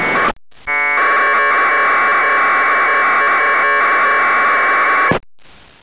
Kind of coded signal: packet radio (AX.25, as in APRS)